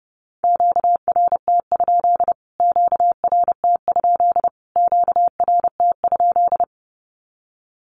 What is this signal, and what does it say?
Morse code, heard as a keyed tone, about 30 wpm, 700 Hz. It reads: QRT? QRT? QRT?
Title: Q-Codes Repeated 3x 30wpm